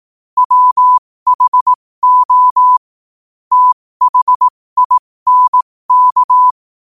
Morse code, heard as a keyed tone, about 18 wpm, spaced as for 15 wpm, 1000 Hz. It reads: WHO THINK
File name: qotdfiles / 15wpm